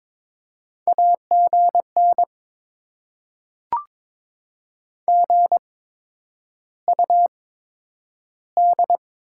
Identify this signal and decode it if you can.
Morse code, heard as a keyed tone, about 22 wpm, spaced as for 9 wpm, 700 Hz. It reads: AGN GUD
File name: Top 20 QSO Elements - 8x Character Spacing - 22wpm